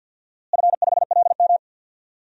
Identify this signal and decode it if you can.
Morse code, heard as a keyed tone, about 47 wpm, 700 Hz. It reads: F5LM
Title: DXCC-100 Call Signs - 4 Characters - Repeated 3x - 50wpm